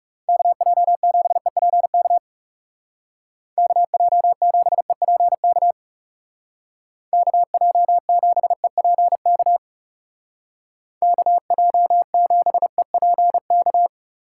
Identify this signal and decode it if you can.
Morse code, heard as a keyed tone, about 36 wpm, 700 Hz. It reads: KJ7EPK KJ7EPK KJ7EPK KJ7EPK
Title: Speed Racing - USA 2x3 Call Signs 30wpm